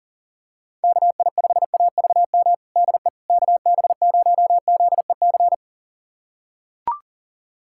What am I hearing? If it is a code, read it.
Morse code, heard as a keyed tone, about 40 wpm, 700 Hz. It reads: KI5AVM DE KB0ZEC
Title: USA Callsign DE Callsign - 6 Characters - 40wpm